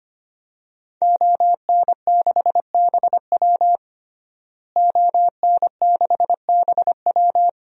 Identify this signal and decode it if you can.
Morse code, heard as a keyed tone, about 25 wpm, 700 Hz. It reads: ON6BW ON6BW
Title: DXCC-100 Call Signs - 5 Characters - Repeated 2x - 25wpm